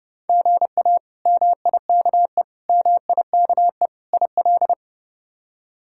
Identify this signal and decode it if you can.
Morse code, heard as a keyed tone, about 30 wpm, 700 Hz. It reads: GA MIKE MIKE IL
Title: K1USN SST QSO Practice - 30wpm